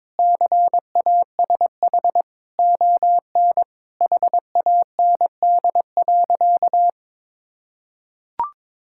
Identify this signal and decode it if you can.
Morse code, heard as a keyed tone, about 22 wpm, 700 Hz. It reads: CASH ON HAND.
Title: Sentences from Top 1000 Words - Review Part 5 - 22wpm